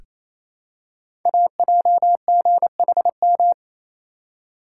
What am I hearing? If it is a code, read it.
Morse code, heard as a keyed tone, about 28 wpm, 700 Hz. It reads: AJGHM
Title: Code Groups - 5 Letters - 28wpm